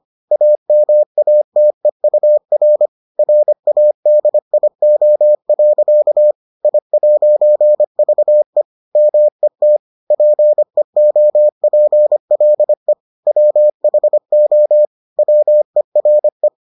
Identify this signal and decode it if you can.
Morse code, heard as a keyed tone, about 25 wpm, 600 Hz. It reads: AMATEUR RADIO. I'VE MET PEOPLE WHO WERE